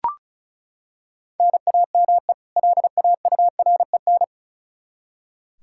Morse code, heard as a keyed tone, about 34 wpm, 700 Hz. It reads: NAME LAUREN
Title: K1USN SST QSO Practice 35wpm